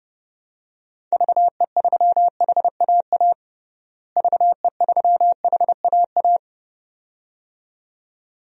Morse code, heard as a keyed tone, about 32 wpm, 700 Hz. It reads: VE3HAA VE3HAA
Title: DXCC-100 Call Signs - 6 Characters - Repeated 2x - 30wpm